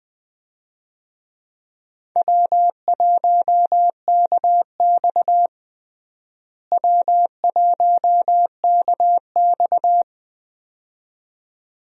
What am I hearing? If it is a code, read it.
Morse code, heard as a keyed tone, about 20 wpm, 700 Hz. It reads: W1KX W1KX
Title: USA Call Signs - 4 Characters - Repeated 2x - 20wpm